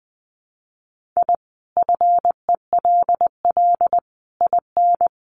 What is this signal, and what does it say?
Morse code, heard as a keyed tone, about 20 wpm, 700 Hz. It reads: I FELL IN
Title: Sets of 5 Words Encore 20wpm